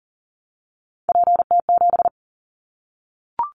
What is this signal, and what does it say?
Morse code, heard as a keyed tone, about 40 wpm, 700 Hz. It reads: PT7
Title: Callsign Prefixes for Top 50 DXCC - 40wpm